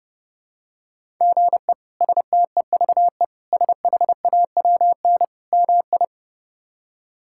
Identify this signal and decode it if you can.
Morse code, heard as a keyed tone, about 30 wpm, 700 Hz. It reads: GE STEVE SHAWN MI
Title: K1USN SST QSO Practice - 30wpm